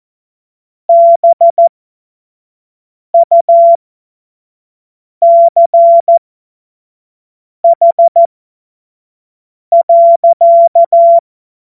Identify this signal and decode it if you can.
Morse code, heard as a keyed tone, about 14 wpm, spaced as for 5 wpm, 680 Hz. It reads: BUCH.
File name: Morsekurs / Witze / Witz 30 Januar 2024 25 F70